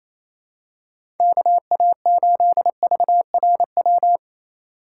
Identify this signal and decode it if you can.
Morse code, heard as a keyed tone, about 28 wpm, 700 Hz. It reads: KA8VRW
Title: USA Call Signs - 6 Characters - 2x Character Spacing - 28wpm